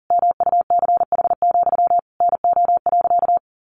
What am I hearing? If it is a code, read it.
Morse code, heard as a keyed tone, about 40 wpm, 700 Hz. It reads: MUCH, NO.
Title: A Sound Of Thunder 40wpm